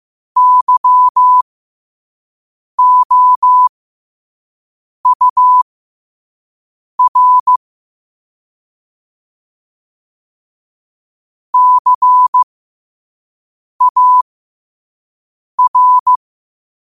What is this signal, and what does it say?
Morse code, heard as a keyed tone, about 15 wpm, spaced as for 5 wpm, 1000 Hz. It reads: YOUR CAR
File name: qotdfiles / 5wpm